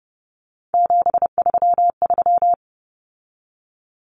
Morse code, heard as a keyed tone, about 30 wpm, 700 Hz. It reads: 733
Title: Common Cut Numbers - 3-Digit 30wpm